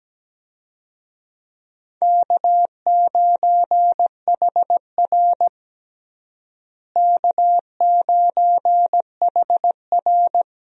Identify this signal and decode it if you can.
Morse code, heard as a keyed tone, about 17 wpm, 700 Hz. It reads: K9HR K9HR
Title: USA Call Signs - 4 Characters - Repeated 3x - 17wpm